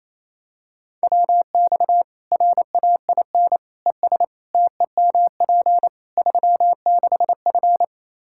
Morse code, heard as a keyed tone, about 28 wpm, 700 Hz. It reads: WX RAIN ES TEMP 36F
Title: Practice QSO - Exchange 2 - Info - 28wpm